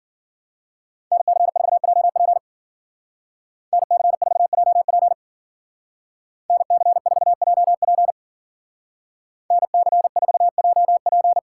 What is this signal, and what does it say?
Morse code, heard as a keyed tone, about 50 wpm, 700 Hz. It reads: NC4JP NC4JP NC4JP NC4JP
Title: Speed Racing - USA 2x2 Call Signs 40wpm